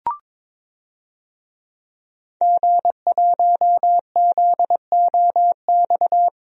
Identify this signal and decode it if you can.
Morse code, heard as a keyed tone, about 22 wpm, 700 Hz. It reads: G1ZOX
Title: DXCC-100 Call Signs - 5 Characters - Repeated 3x - 22wpm